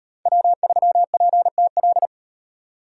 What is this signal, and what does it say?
Morse code, heard as a keyed tone, about 38 wpm, 700 Hz. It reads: W3PTL